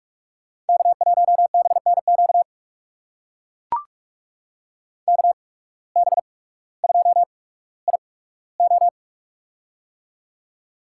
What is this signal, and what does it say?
Morse code, heard as a keyed tone, about 44 wpm, spaced as for 19 wpm, 700 Hz. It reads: K1BNQ KB2IO
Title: USA Call Signs - 5 Characters - 8x Character Spacing - 45wpm